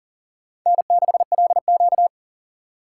Morse code, heard as a keyed tone, about 38 wpm, 700 Hz. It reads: N6LQ
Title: USA Call Signs - 4 Characters - 4x Character Spacing - 40wpm